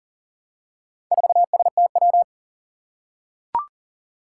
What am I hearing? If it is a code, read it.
Morse code, heard as a keyed tone, about 40 wpm, 700 Hz. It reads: 4STW
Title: Four Challenging Letter-Number Combinations 40wpm